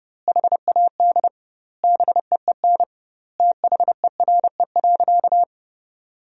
Morse code, heard as a keyed tone, about 32 wpm, 700 Hz. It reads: HAD BEEN THERE.
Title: 2x Word-Spacing - Sentences from Top 300 Words - 30wpm